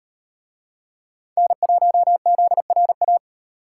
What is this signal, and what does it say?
Morse code, heard as a keyed tone, about 35 wpm, 700 Hz. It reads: N1ZRA